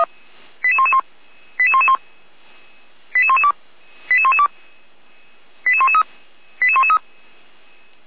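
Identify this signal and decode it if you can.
Five-tone selective call (ZVEI), heard as quick runs of five tones, tones of about 70 ms. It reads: 88111 88111 88112 88112 88113 88113
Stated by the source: ZVEI five-tone selective call 88497, received on an SDR, from the feuerwehr-orlamuende fire brigade audio server.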